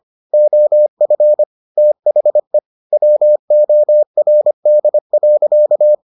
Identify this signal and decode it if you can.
Morse code, heard as a keyed tone, about 25 wpm, 600 Hz. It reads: OF THE WORD.